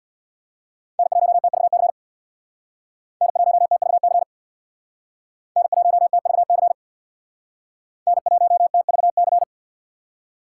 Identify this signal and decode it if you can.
Morse code, heard as a keyed tone, about 66 wpm, 700 Hz. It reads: N1TVC N1TVC N1TVC N1TVC